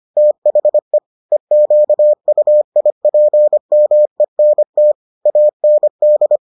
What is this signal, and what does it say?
Morse code, heard as a keyed tone, about 25 wpm, 600 Hz. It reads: THE EQUIPMENT AND